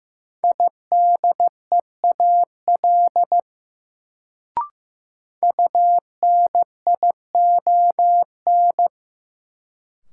Morse code, heard as a keyed tone, about 15 wpm, 700 Hz. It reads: IDEAL UNION
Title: Sentences from Top 1400 Words - New Word Drill - 15wpm